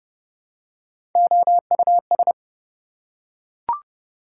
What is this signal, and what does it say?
Morse code, heard as a keyed tone, about 29 wpm, 700 Hz. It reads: OUS
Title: Prefixes and Suffixes 30wpm